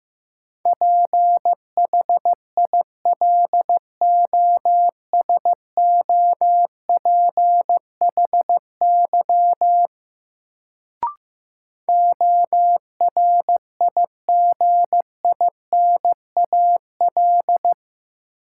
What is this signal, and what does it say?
Morse code, heard as a keyed tone, about 15 wpm, 700 Hz. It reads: PHILOSOPHY ORIGINAL
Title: Sentences from Top 900 Words - New Word Drill - 15wpm